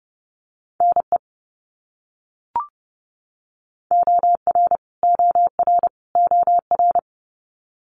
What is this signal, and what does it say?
Morse code, heard as a keyed tone, about 30 wpm, 700 Hz. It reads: NE OR OR OR
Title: US States Repeated 3x 30wpm